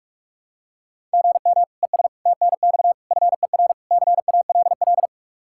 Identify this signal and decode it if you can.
Morse code, heard as a keyed tone, about 46 wpm, 700 Hz. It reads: GM ES TNX FER CALL
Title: K1USN SST QSO Practice 45wpm